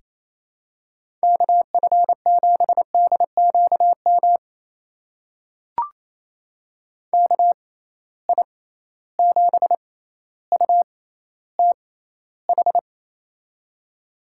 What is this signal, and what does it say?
Morse code, heard as a keyed tone, about 28 wpm, spaced as for 14 wpm, 700 Hz. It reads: KF7DQM KI7UTH
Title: USA Call Signs - 6 Characters - 6x Character Spacing - 28wpm